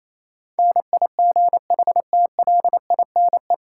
Morse code, heard as a keyed tone, about 27 wpm, 700 Hz. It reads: NIGHTLINE
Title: Sets of 5 Words - New Word Drill - 28wpm